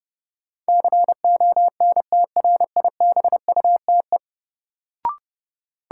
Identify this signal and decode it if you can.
Morse code, heard as a keyed tone, about 30 wpm, 700 Hz. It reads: CONTRIBUTE
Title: Sentences from Top 1400 Words - New Word Drill Alt - 30wpm